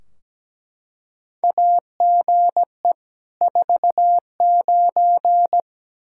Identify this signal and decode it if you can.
Morse code, heard as a keyed tone, about 17 wpm, 700 Hz. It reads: AGE 49
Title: K1USN SST QSO Practice 17wpm